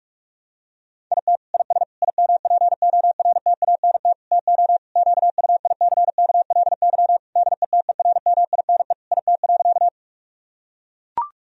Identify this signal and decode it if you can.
Morse code, heard as a keyed tone, about 43 wpm, 700 Hz. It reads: IT IS IMPORTANT TO QUICKLY DETERMINE IT.
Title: Sentences from Top 1000 Words - Review Part 4 - 45wpm